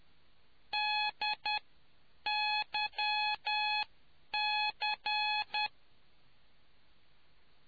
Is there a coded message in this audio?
Yes: Morse code from 0.7 to 5.7 s, heard as a keyed tone, about 10 wpm, spaced as for 8 wpm, 840 Hz. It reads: DYC